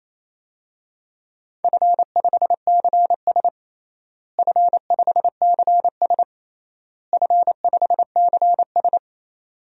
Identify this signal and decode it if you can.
Morse code, heard as a keyed tone, about 28 wpm, 700 Hz. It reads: F5CS F5CS F5CS
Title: DXCC-100 Call Signs - 4 Characters - Repeated 3x - 28wpm